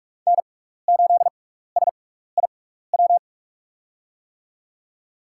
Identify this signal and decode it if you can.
Morse code, heard as a keyed tone, about 42 wpm, spaced as for 15 wpm, 700 Hz. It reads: N8SIW
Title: USA Call Signs - 6x Character Spacing - 45wpm